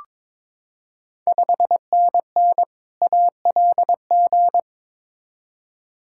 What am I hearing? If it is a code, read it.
Morse code, heard as a keyed tone, about 22 wpm, 700 Hz. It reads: 5NN ALG
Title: Ontario QSO Party Exchange 22wpm